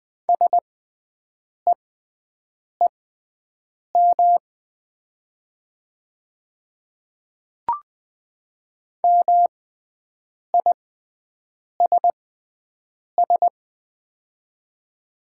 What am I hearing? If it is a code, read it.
Morse code, heard as a keyed tone, about 21 wpm, spaced as for 7 wpm, 700 Hz. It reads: SEEM MISS
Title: 6x Character Spacing - Top 500 Words - 20wpm